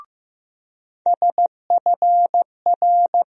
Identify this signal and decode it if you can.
Morse code, heard as a keyed tone, about 15 wpm, 700 Hz. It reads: SFR
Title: USA Call Sign Postfixes - 3 Letters 15wpm